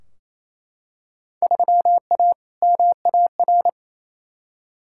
Morse code, heard as a keyed tone, about 28 wpm, 700 Hz. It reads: 3A MAR
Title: Field Day Exchanges Encore - 28wpm